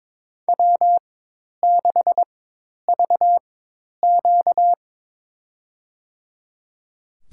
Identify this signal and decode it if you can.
Morse code, heard as a keyed tone, about 22 wpm, spaced as for 10 wpm, 700 Hz. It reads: W6VQ
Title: USA Call Signs - 4 Characters - 4x Character Spacing - 22wpm